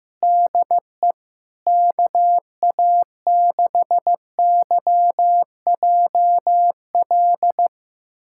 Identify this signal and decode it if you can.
Morse code, heard as a keyed tone, about 15 wpm, 700 Hz. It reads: DE KA6YJL
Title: USA Callsign DE Callsign - 6 Characters - 15wpm